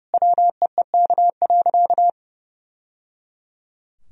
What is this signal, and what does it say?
Morse code, heard as a keyed tone, about 30 wpm, 700 Hz. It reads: WEEK.
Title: Sentences from Top 1000 Words - Review Part 4 - 30wpm